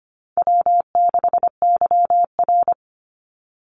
Morse code, heard as a keyed tone, about 25 wpm, 700 Hz. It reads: W6YR
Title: Call Signs Repeated 3x - 25wpm